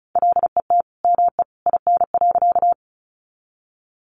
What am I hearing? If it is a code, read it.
Morse code, heard as a keyed tone, about 34 wpm, 700 Hz. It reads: LET ME IN.
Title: Sentences from Top 300 Words 35wpm